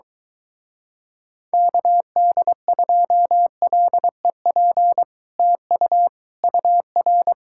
Morse code, heard as a keyed tone, about 23 wpm, 700 Hz. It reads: KD2LEP TU UR